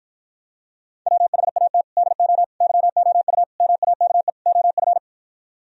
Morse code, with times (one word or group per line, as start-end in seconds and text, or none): 1.1-1.8 WHAT
2.0-2.4 DO
2.6-3.4 YOU
3.6-4.3 MAKE
4.5-5.0 OF